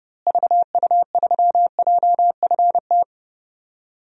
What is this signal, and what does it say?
Morse code, heard as a keyed tone, about 30 wpm, 700 Hz. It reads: VU3JFT
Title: DXCC-100 Call Signs - 6 Characters - Repeated 3x - 30wpm